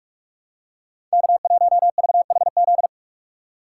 Morse code, heard as a keyed tone, about 44 wpm, 700 Hz. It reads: K1VHZ
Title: USA Call Signs - 5 Characters - 6x Character Spacing - 45wpm